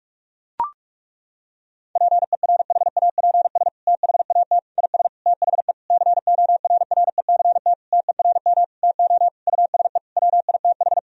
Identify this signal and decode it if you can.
Morse code, heard as a keyed tone, about 41 wpm, 700 Hz. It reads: PERHAPS THAT IS THE CORRECT TERM TO USE WITH
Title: Sentences from Top 1000 Words - Review Part 4 - 45wpm